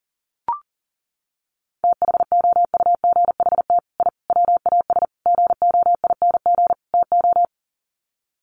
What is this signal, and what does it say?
Morse code, heard as a keyed tone, about 40 wpm, 700 Hz. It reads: THOUGHT I WAS GOING TO
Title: Sets of 5 Words Encore2 40wpm